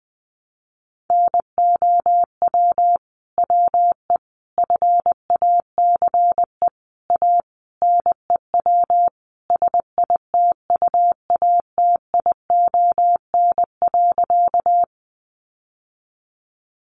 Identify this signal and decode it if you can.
Morse code, heard as a keyed tone, about 20 wpm, 700 Hz. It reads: NOW WE FACE A NEW SITUATION.
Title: Sentences from Top 1200 Words 20wpm